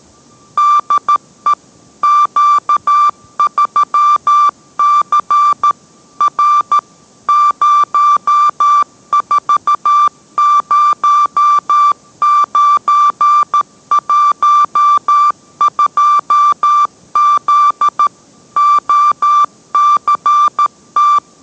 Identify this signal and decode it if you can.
Morse code, heard as a keyed tone, about 13 wpm, 1170 Hz. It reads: DE Q3C R 040912Z OCT